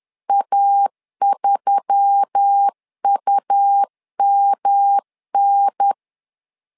Morse code, heard as a keyed tone, about 10 wpm, 800 Hz. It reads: A3UMN
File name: LEZIONI / 4 Q1S3 10wpm 2427